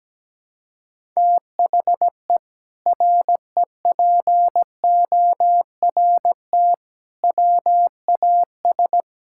Morse code, heard as a keyed tone, about 17 wpm, 700 Hz. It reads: THE REPORT WAS
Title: Sentences from Top 1000 Words - Review Part 6 - 17wpm